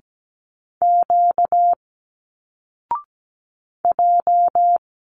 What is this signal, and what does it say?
Morse code, heard as a keyed tone, about 17 wpm, 700 Hz. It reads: Q J
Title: Single Letters 17wpm